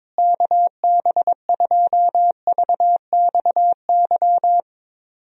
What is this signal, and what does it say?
Morse code, heard as a keyed tone, about 21 wpm, 700 Hz. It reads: KB2VXY
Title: USA Call Signs - 6 Characters - Repeated 3x - 22wpm